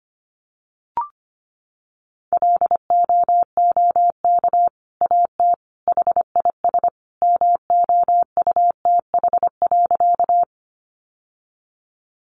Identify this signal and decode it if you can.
Morse code, heard as a keyed tone, about 25 wpm, 700 Hz. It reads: LOOK AT HIS MOUTH.
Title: Sentences from Top 1500 Words 25wpm